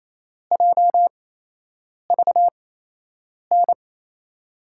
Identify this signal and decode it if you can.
Morse code, heard as a keyed tone, about 28 wpm, spaced as for 8 wpm, 700 Hz. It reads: JVN
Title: USA Call Signs - 8x Character Spacing - 28wpm